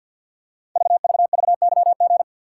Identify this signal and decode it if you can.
Morse code, heard as a keyed tone, about 50 wpm, 700 Hz. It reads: VVVYG
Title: Code Groups - 5 Letters - 50wpm